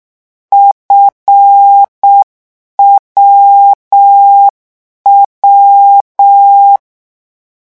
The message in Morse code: FWW